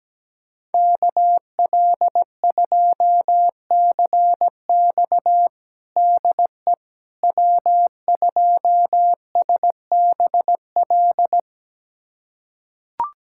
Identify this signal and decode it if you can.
Morse code, heard as a keyed tone, about 17 wpm, 700 Hz. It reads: KL2CX DE W2SBL
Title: USA Callsign DE Callsign - 5 Characters - 17wpm